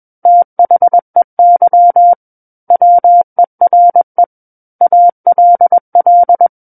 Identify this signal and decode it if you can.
Morse code, heard as a keyed tone, about 21 wpm, 700 Hz. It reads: THEY WERE ALL
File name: Book One / 12-Chapter 12 - 21 WPM